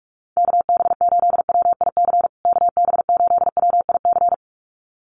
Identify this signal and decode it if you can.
Morse code, heard as a keyed tone, about 44 wpm, 700 Hz. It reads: KB8WIC KB8WIC
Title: USA Call Signs Repeated 2x 45wpm